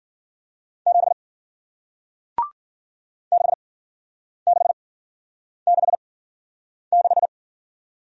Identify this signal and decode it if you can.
Morse code, heard as a keyed tone, about 53 wpm, spaced as for 9 wpm, 700 Hz. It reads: 7 6666